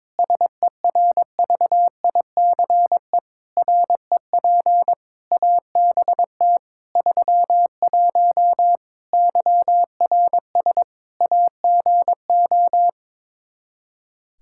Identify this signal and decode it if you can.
Morse code, heard as a keyed tone, about 22 wpm, 700 Hz. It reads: SERVICE REP ABT 31 YRS AGO
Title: Practice QSO - Exchange 3 - More Info - 22wpm